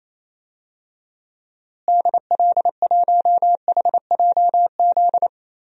Morse code, heard as a keyed tone, about 28 wpm, 700 Hz. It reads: DL1HJZ
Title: DXCC-100 Call Signs - 6 Characters - Repeated 3x - 28wpm